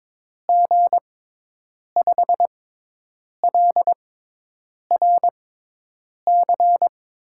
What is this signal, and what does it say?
Morse code, heard as a keyed tone, about 21 wpm, spaced as for 8 wpm, 700 Hz. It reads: G5LRC
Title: USA Call Signs - 6 Characters - 6x Character Spacing - 22wpm